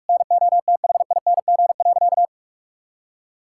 Morse code, heard as a keyed tone, about 42 wpm, 700 Hz. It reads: NOTHING.
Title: Sentences from Top 1000 Words - Review Part 6 - 45wpm